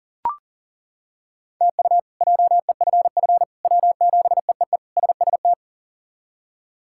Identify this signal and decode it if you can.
Morse code, heard as a keyed tone, about 40 wpm, 700 Hz. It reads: TU JEFF W7EEE SST
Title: K1USN SST QSO Practice - 40wpm